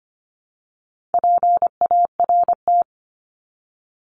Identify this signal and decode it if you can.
Morse code, heard as a keyed tone, about 25 wpm, 700 Hz. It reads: PART
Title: ES:S Suffix - Top 100 Words with Base Words 25wpm